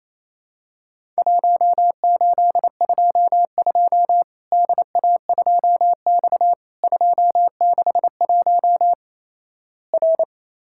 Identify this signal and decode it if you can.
Morse code, heard as a keyed tone, about 28 wpm, 700 Hz. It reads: 1822 DA2X 261 R
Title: Worked All Europe DX Contest - QTC Exchange - 28wpm